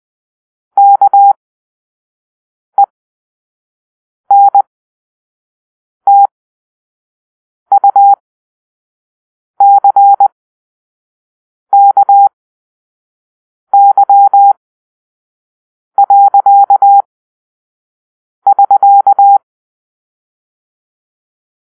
Morse code, heard as a keyed tone, about 20 wpm, spaced as for 5 wpm, 800 Hz. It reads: KENTUCKY.<SK>
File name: Grp-5 Drill-2